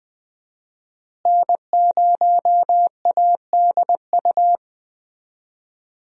Morse code, heard as a keyed tone, about 20 wpm, 700 Hz. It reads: N0ADU
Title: USA Call Signs - 5 Characters - Repeated 2x - 20wpm